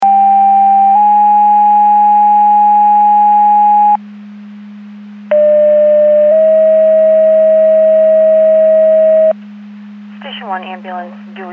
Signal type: two-tone page